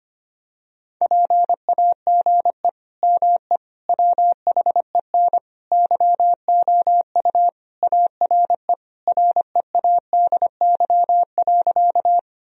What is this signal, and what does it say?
Morse code, heard as a keyed tone, about 25 wpm, 700 Hz. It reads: PAGE ME WHEN YOU ARE READY.
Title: Sentences from Top 1000 Words - Review Part 1 - 25wpm